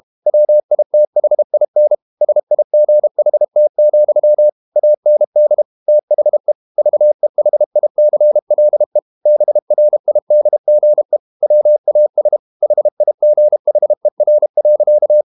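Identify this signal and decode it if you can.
Morse code, heard as a keyed tone, about 32 wpm, 600 Hz. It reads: WITHIN SIGHT, AND THE VEHICLE BRIDGE WAS HIGHER.
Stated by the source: Morse code practice at 32 wpm from CW med Gnister, reading Ken Follett's novel Never in English.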